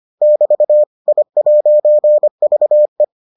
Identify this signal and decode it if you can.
Morse code, heard as a keyed tone, about 25 wpm, 600 Hz. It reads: <BT> I'VE